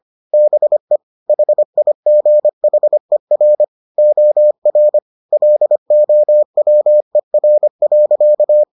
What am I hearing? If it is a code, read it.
Morse code, heard as a keyed tone, about 25 wpm, 600 Hz. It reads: BE HIGHER OR LOWER.